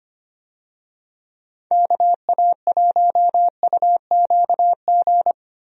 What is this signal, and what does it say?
Morse code, heard as a keyed tone, about 25 wpm, 700 Hz. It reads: KA1UQG
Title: USA Call Signs - 6 Characters - Repeated 3x - 25wpm